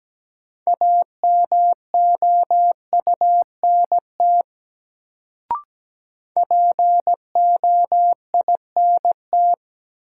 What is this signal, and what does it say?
Morse code, heard as a keyed tone, about 17 wpm, 700 Hz. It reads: AMOUNT POINT
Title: Sets of 3 Words - New Word Drill - 17wpm